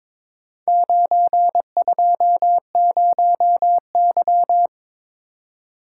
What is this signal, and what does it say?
Morse code, heard as a keyed tone, about 23 wpm, 700 Hz. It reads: 920Y
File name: practice / 22wpm / Four Letter-Number Combinations - 22wpm